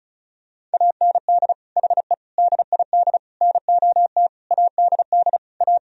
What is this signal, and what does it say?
Morse code, heard as a keyed tone, about 36 wpm, 700 Hz. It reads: AND HE DID NOT ADD A
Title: Sentences from Top 1000 Words - Review Part 1 - 35wpm